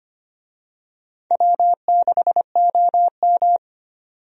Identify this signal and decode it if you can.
Morse code, heard as a keyed tone, about 25 wpm, 700 Hz. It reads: W6OM
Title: USA Call Signs - 4 Characters - 25wpm